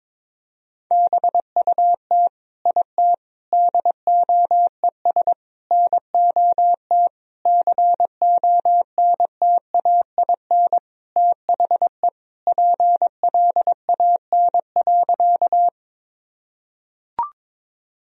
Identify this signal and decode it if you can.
Morse code, heard as a keyed tone, about 21 wpm, 700 Hz. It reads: BUT IT DOES NOT CONTAIN THE PLAN.
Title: Sentences from Top 1000 Words - Review Part 2 - 22wpm